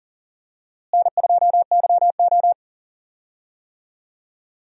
Morse code, heard as a keyed tone, about 40 wpm, 700 Hz. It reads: N2YO
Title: USA Call Signs - 4 Characters - Repeated 3x - 40wpm